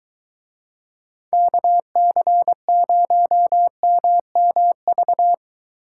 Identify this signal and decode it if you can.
Morse code, heard as a keyed tone, about 21 wpm, 700 Hz. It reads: KC0MMV